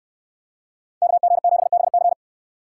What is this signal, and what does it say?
Morse code, heard as a keyed tone, about 68 wpm, 700 Hz. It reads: KG7BC